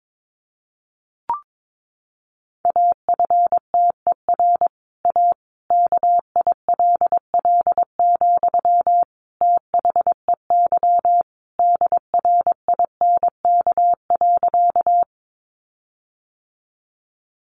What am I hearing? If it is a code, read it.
Morse code, heard as a keyed tone, about 22 wpm, 700 Hz. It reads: AFTER A KILL, THEY DRINK.
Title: Sentences from Top 1500 Words 22wpm